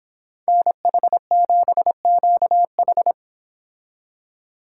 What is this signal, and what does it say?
Morse code, heard as a keyed tone, about 25 wpm, 700 Hz. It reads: NH7QH